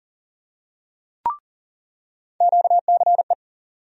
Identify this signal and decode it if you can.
Morse code, heard as a keyed tone, about 38 wpm, 700 Hz. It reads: QCE
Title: USA Call Sign Postfixes - 3 Letters 40wpm